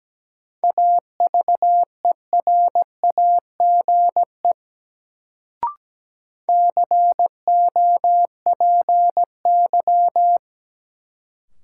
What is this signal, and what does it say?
Morse code, heard as a keyed tone, about 17 wpm, 700 Hz. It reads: AVERAGE COPY
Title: Sentences from Top 1400 Words - New Word Drill Alt - 17wpm